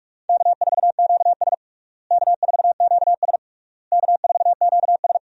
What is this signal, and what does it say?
Morse code, heard as a keyed tone, about 46 wpm, 700 Hz. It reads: K4QS K4QS K4QS
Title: USA Call Signs - 4 Characters - Repeated 3x - 45wpm